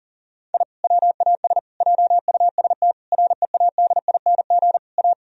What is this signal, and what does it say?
Morse code, heard as a keyed tone, about 40 wpm, 700 Hz. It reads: I WAS JUST READING A
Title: Sentences from Top 1000 Words - Review Part 3 - 40wpm